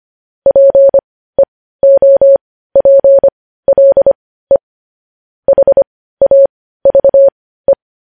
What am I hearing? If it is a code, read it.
Morse code, heard as a keyed tone, about 25 wpm, spaced as for 15 wpm, 550 Hz. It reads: PEOPLE HAVE
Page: news headlines in morse code at 15 wpm